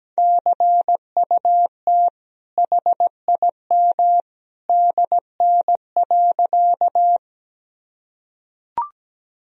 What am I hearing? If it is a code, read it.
Morse code, heard as a keyed tone, about 17 wpm, 700 Hz. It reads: CUT HIM DN.
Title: Sentences from Top 300 Words and Abbreviations 17wpm